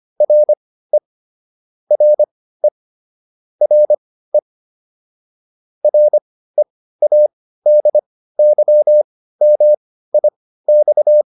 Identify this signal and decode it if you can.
Morse code, heard as a keyed tone, about 25 wpm, spaced as for 15 wpm, 600 Hz. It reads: RE RE RE READYMIX